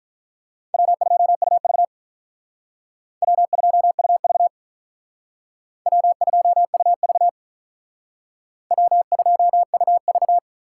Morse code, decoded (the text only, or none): W2UV W2UV W2UV W2UV